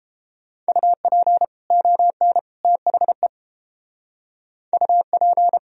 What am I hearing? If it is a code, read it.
Morse code, heard as a keyed tone, about 30 wpm, 700 Hz. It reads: UP ON THE UP